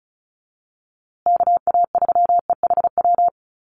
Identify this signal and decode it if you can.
Morse code, heard as a keyed tone, about 34 wpm, 700 Hz. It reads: KA3EHW
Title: USA Call Signs Repeated 3x 35wpm